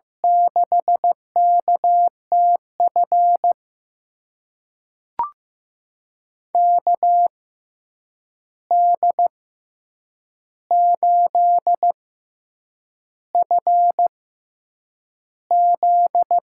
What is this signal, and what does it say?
Morse code, heard as a keyed tone, about 15 wpm, spaced as for 7 wpm, 700 Hz. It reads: 6KTF KD8FZ
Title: USA Call Signs - 6x Character Spacing - 15wpm